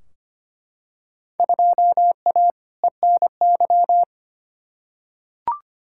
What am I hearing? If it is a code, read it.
Morse code, heard as a keyed tone, about 25 wpm, 700 Hz. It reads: 2A ENY
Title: Field Day Exchanges Encore - 25wpm